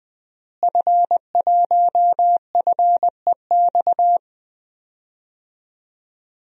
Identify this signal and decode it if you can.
Morse code, heard as a keyed tone, about 20 wpm, 700 Hz. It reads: F1FEX
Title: DXCC-100 Call Signs - 5 Characters - Repeated 2x - 20wpm